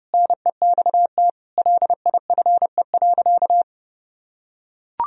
Morse code, heard as a keyed tone, about 30 wpm, 700 Hz. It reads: NEXT LIFE.
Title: Sentences from Top 1000 Words - Review Part 1 - 30wpm